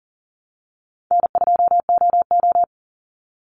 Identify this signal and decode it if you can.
Morse code, heard as a keyed tone, about 40 wpm, 700 Hz. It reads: N2OO
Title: Easy Call Signs Repeated 2x - 40wpm